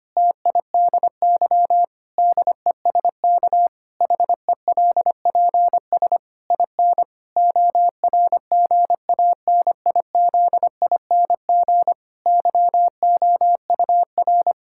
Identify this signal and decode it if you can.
Morse code, heard as a keyed tone, about 25 wpm, 700 Hz. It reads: TIDY DESK HELPS IN ORGANIZING YOUR
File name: practice / 25wpm / Difficult Sentences with Hints - 25wpm